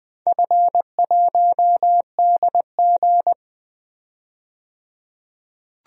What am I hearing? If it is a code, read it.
Morse code, heard as a keyed tone, about 20 wpm, 700 Hz. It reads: F1DG
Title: DXCC-100 Call Signs - 4 Characters - 20wpm